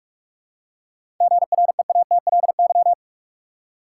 Morse code, heard as a keyed tone, about 46 wpm, 700 Hz. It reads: GREATLY
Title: Seven Letter Words - 45wpm